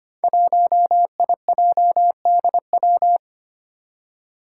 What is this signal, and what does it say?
Morse code, heard as a keyed tone, about 25 wpm, 700 Hz. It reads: 1IJDW
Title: Five Challenging Letter-Number Combinations 25wpm